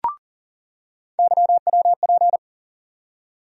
Morse code, heard as a keyed tone, about 38 wpm, 700 Hz. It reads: YWP
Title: USA Call Sign Postfixes - 3 Letters 40wpm